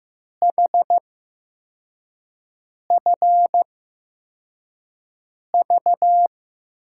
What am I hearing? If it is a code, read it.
Morse code, heard as a keyed tone, about 15 wpm, spaced as for 4 wpm, 700 Hz. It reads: HFV
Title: USA Call Signs - 5 Characters - 8x Character Spacing - 15wpm